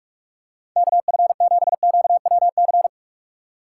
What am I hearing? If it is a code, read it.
Morse code, heard as a keyed tone, about 46 wpm, 700 Hz. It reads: KF7QWC
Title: USA Call Signs - 3x Character Spacing - 45wpm